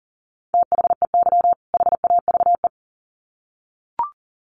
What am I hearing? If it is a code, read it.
Morse code, heard as a keyed tone, about 40 wpm, 700 Hz. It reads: THEY HAVE
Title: Sets of 2 Words 40wpm